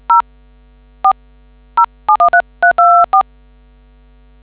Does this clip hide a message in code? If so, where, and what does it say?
Yes: DTMF from 0.1 to 3.2 s, heard as telephone touch tones, tones of about 110 ms. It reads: 0 4 0713324